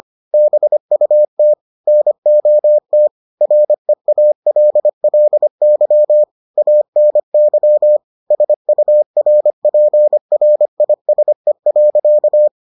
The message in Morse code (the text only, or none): BUT NOT REALLY ANY SURPRISE.